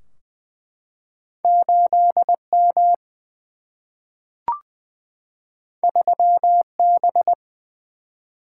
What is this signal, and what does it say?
Morse code, heard as a keyed tone, about 20 wpm, 700 Hz. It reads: 8 M 3 B
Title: Single-Letter Single-Number 20wpm